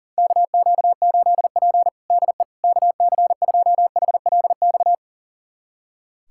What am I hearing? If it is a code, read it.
Morse code, heard as a keyed tone, about 40 wpm, 700 Hz. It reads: KQ8P DE KC2HLX
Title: USA Callsign DE Callsign 40wpm